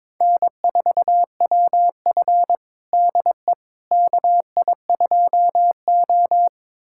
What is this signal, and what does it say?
Morse code, heard as a keyed tone, about 22 wpm, 700 Hz. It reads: N4WF DE KI2O
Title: USA Callsign DE Callsign - 4 Characters - 22wpm